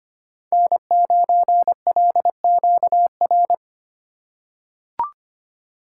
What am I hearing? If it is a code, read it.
Morse code, heard as a keyed tone, about 25 wpm, 700 Hz. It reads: N9LQR
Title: USA Call Signs - 3x Character Spacing - 25wpm